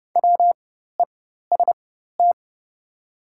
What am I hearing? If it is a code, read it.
Morse code, heard as a keyed tone, about 29 wpm, spaced as for 14 wpm, 690 Hz. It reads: WEST
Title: 4x Character Spacing - Top 500 Words - 30wpm